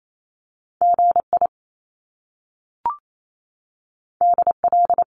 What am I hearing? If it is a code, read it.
Morse code, heard as a keyed tone, about 28 wpm, 700 Hz. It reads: GI DL
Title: Top Two Letter Combinations in Words 28wpm